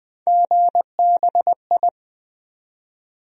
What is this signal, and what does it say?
Morse code, heard as a keyed tone, about 20 wpm, 700 Hz. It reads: GBI
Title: USA Call Sign Postfixes - 3 Letters 20wpm